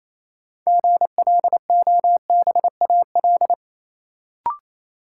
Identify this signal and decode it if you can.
Morse code, heard as a keyed tone, about 28 wpm, 700 Hz. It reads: GLOBAL
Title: Sentences from Top 1200 Words - New Word Drill Alt - 28wpm